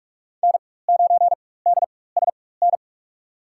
Morse code, heard as a keyed tone, about 42 wpm, spaced as for 21 wpm, 700 Hz. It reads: N9DSN
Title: USA Call Signs - 5 Characters - 4x Character Spacing - 45wpm